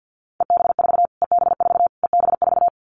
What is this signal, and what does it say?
Morse code, heard as a keyed tone, about 50 wpm, 700 Hz. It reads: EB4 EB4 EB4
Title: Callsign Prefixes for Top 25 DXCC - Repeated 3x - 50wpm